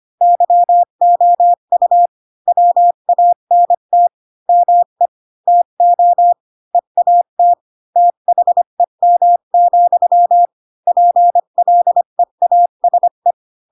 Morse code, heard as a keyed tone, about 25 wpm, 700 Hz. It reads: YOU WANT ME TO EAT THEM, PLEASE
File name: Pinocchio / Pinocchio ch7 20 25 ik1hge